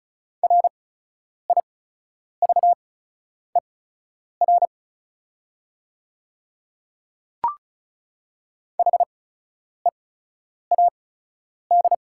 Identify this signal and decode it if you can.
Morse code, heard as a keyed tone, about 36 wpm, spaced as for 10 wpm, 690 Hz. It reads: RIVER HEAD
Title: 8x Character Spacing - Top 500 Words - 35wpm